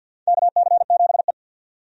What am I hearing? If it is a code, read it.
Morse code, heard as a keyed tone, about 52 wpm, 700 Hz. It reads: KC7E
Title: USA Call Signs - 4 Characters - 50wpm